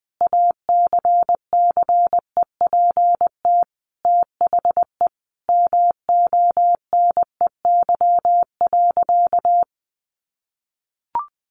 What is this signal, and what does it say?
Morse code, heard as a keyed tone, about 20 wpm, 700 Hz. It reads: ACCEPT THE MONEY.
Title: Sentences from Top 1300 Words 20wpm